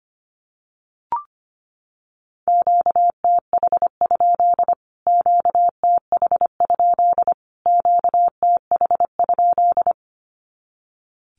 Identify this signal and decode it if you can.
Morse code, heard as a keyed tone, about 25 wpm, 700 Hz. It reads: QTH? QTH? QTH?
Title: Q-Codes Repeated 3x 25wpm